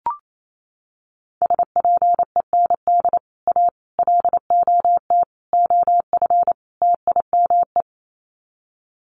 Morse code, heard as a keyed tone, about 28 wpm, 700 Hz. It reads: SPEND A LOT OF TIME
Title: Sets of 5 Words 28wpm